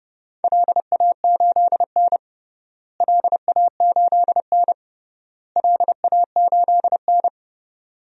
Morse code, heard as a keyed tone, about 30 wpm, 700 Hz. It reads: LA8N LA8N LA8N
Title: DXCC-100 Call Signs - 4 Characters - Repeated 3x - 30wpm